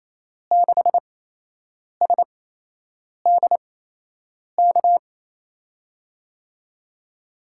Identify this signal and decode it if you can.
Morse code, heard as a keyed tone, about 27 wpm, spaced as for 8 wpm, 700 Hz. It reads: 6SDK
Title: USA Call Signs - 5 Characters - 8x Character Spacing - 28wpm